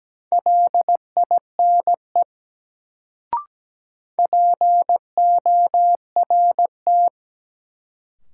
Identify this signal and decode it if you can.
Morse code, heard as a keyed tone, about 17 wpm, 700 Hz. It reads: LINE PORT
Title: Sentences from Top 1000 Words - Review Part 1 - New Word Drill - 17wpm